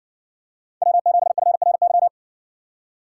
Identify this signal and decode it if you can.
Morse code, heard as a keyed tone, about 60 wpm, 700 Hz. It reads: W7FRY